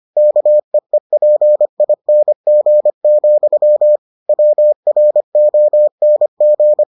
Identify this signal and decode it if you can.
Morse code, heard as a keyed tone, about 25 wpm, 600 Hz. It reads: KEEPING, WRONG